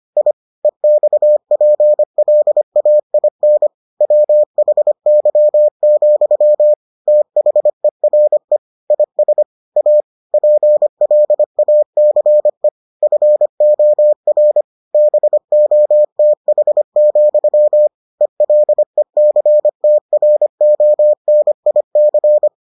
Morse code, heard as a keyed tone, about 25 wpm, 600 Hz. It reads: I EXPLAIN WHY, THERE IS A PLACE FOR BOTH, ELECTRONIC